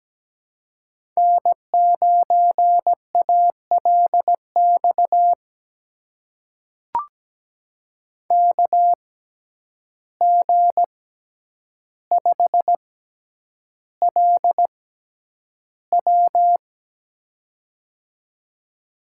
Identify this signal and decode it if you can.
Morse code, heard as a keyed tone, about 17 wpm, spaced as for 9 wpm, 700 Hz. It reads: N9ALX KG5LW
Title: USA Call Signs - 5 Characters - 6x Character Spacing - 17wpm